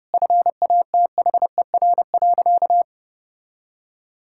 Morse code, heard as a keyed tone, about 32 wpm, 700 Hz. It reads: FATHER.
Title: Sentences from Top 1000 Words - Review Part 5 - 30wpm